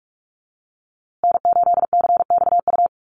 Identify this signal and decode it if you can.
Morse code, heard as a keyed tone, about 44 wpm, 700 Hz. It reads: N8CXU
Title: USA Call Signs Repeated 2x 45wpm